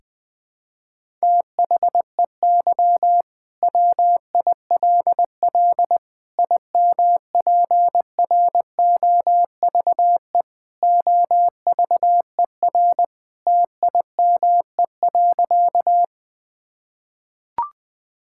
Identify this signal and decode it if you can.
Morse code, heard as a keyed tone, about 20 wpm, 700 Hz. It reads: THEY WILL IMPROVE OVER TIME.
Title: Sentences from Top 1000 Words - Review Part 5 - 20wpm